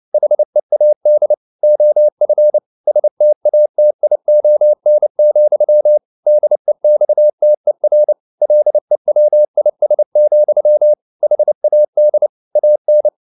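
Morse code, heard as a keyed tone, about 29 wpm, 600 Hz. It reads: HEAD OF STATION, DEXTER LEWIS, HAD AN
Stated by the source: Morse code at 29 wpm, sending Ken Follett's novel Never in English, from CW med Gnister.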